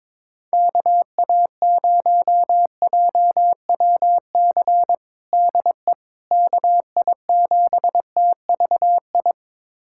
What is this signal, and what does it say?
Morse code, heard as a keyed tone, about 22 wpm, 700 Hz. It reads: KA0JWC DE KI7TVI
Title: USA Callsign DE Callsign - 6 Characters - 22wpm